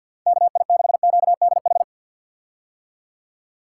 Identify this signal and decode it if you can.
Morse code, heard as a keyed tone, about 46 wpm, 700 Hz. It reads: KI6QDH
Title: K1USN SST QSO Practice - 50wpm